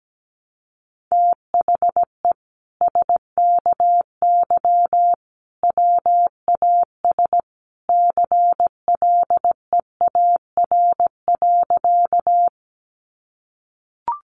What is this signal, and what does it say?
Morse code, heard as a keyed tone, about 17 wpm, 700 Hz. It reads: THE SKY WAS CLEAR.
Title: Sentences from Top 1200 Words 17wpm